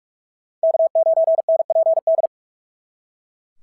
Morse code, heard as a keyed tone, about 42 wpm, 650 Hz. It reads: K9NPD
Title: SOTA QSO Practice 45wpm